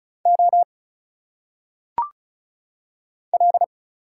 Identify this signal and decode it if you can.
Morse code, heard as a keyed tone, about 36 wpm, 700 Hz. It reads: O L
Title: Single Letters - Phonetic Pronunciation 35wpm